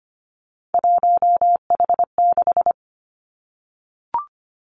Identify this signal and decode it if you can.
Morse code, heard as a keyed tone, about 25 wpm, 700 Hz. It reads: 1H6
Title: Three Challenging Letter-Number Combinations 25wpm